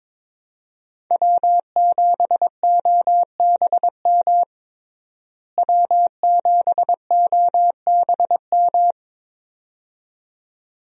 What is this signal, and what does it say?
Morse code, heard as a keyed tone, about 22 wpm, 700 Hz. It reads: W7OBM W7OBM
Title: USA Call Signs - 5 Characters - Repeated 2x - 22wpm